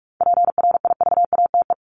Morse code, heard as a keyed tone, about 44 wpm, 700 Hz. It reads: PRIVATE
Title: Top 751-1000 Words 45wpm